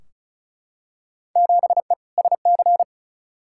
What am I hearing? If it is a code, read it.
Morse code, heard as a keyed tone, about 36 wpm, 700 Hz. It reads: 7E SC
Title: Field Day Exchanges Encore - 35wpm